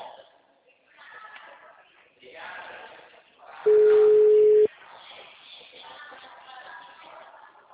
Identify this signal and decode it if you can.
Ringback tone (Europe, 425 Hz), with speech.
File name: out- -2002- - - .82239